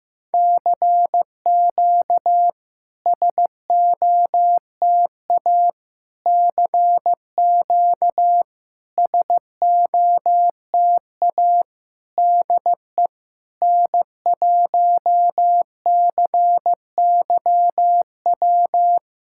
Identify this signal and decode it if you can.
Morse code, heard as a keyed tone, about 15 wpm, 690 Hz. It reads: CQ SOTA CQ SOTA DE N1CYW
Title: SOTA QSO Practice 15wpm